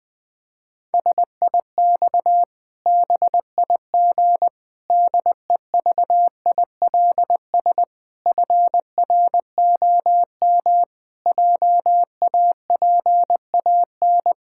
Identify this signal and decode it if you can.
Morse code, heard as a keyed tone, about 21 wpm, 700 Hz. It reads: SIX BIG DEVILS FROM JAPAN
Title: Pangram Challenge v2 - 20wpm